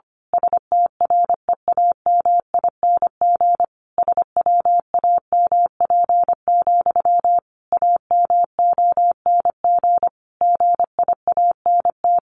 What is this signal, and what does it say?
Morse code, heard as a keyed tone, about 25 wpm, 700 Hz. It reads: STREAMING SWAMP, AMONG GIANT
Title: A Sound Of Thunder 25wpm